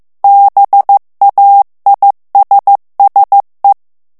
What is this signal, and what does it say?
Morse code, heard as a keyed tone, about 15 wpm, 800 Hz. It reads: BAISSE
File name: morse mp3 / FRANCAIS TEXTE CLAIR 1 VITESSE 14 MOTS MINUTES